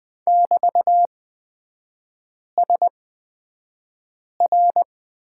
Morse code, heard as a keyed tone, about 20 wpm, spaced as for 5 wpm, 700 Hz. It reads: <BT>SR
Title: 243 Lesson 39e - Letter z - Three Characters 20wpm